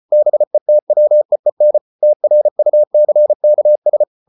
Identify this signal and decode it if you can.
Morse code, heard as a keyed tone, about 34 wpm, 600 Hz. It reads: BETWEEN TRUCKS